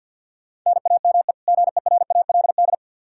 Morse code, heard as a keyed tone, about 48 wpm, 700 Hz. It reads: NAME GERALD
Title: Practice QSO - Exchange 1 - Beginning - 50wpm